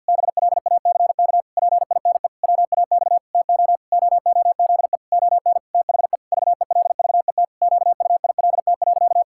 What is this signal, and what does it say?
Morse code, heard as a keyed tone, about 50 wpm, 700 Hz. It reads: BLACK PINE WAX TO OOZE ON THE VELVET QUILT.
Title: Pangram Challenge v2 - 50wpm